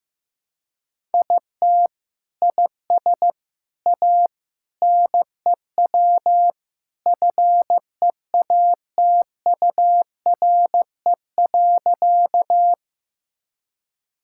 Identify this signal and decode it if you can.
Morse code, heard as a keyed tone, about 16 wpm, 700 Hz. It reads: IT IS A NEW FEATURE.
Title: Sentences from Top 1000 Words - Review Part 5 - 15wpm